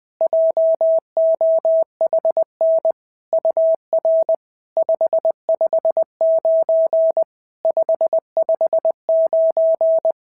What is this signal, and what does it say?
Morse code, heard as a keyed tone, about 19 wpm, 650 Hz. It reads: JOHN UR 559 559
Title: SOTA QSO Practice 20wpm